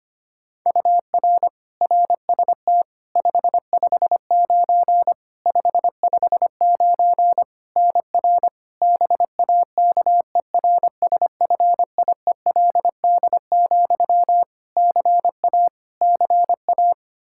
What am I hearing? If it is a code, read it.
Morse code, heard as a keyed tone, about 25 wpm, 700 Hz. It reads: UR RST 559 559 NR BAKERSFIELD, CA CA
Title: Practice QSO - Exchange 1 - Beginning - 25wpm